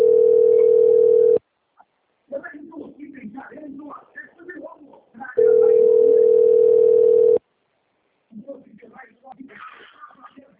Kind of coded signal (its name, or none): telephone call progress tone